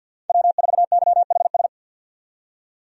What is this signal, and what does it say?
Morse code, heard as a keyed tone, about 51 wpm, 700 Hz. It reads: W4YHS
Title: USA Call Signs - 5 Characters - 50wpm